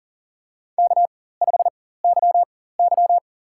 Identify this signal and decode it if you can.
Morse code, heard as a keyed tone, about 40 wpm, spaced as for 19 wpm, 700 Hz. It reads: K5YY
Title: USA Call Signs - 4 Characters - 4x Character Spacing - 40wpm